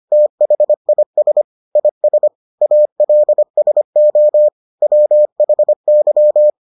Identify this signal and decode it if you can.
Morse code, heard as a keyed tone, about 25 wpm, 600 Hz. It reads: THIS IS ALSO WHY